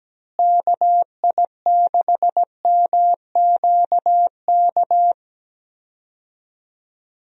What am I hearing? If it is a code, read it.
Morse code, heard as a keyed tone, about 17 wpm, 700 Hz. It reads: KI6MQK
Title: USA Call Signs - 6 Characters - Repeated 3x - 17wpm